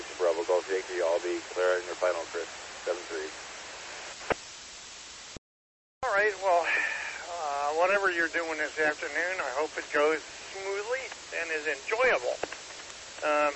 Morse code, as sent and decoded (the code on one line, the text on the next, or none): none